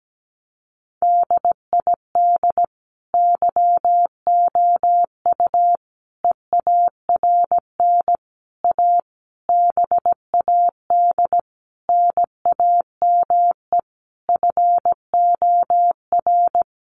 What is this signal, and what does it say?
Morse code, heard as a keyed tone, about 17 wpm, 700 Hz. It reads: DID YOU EARN A BAD NAME FOR
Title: Sentences from Top 1500 Words 17wpm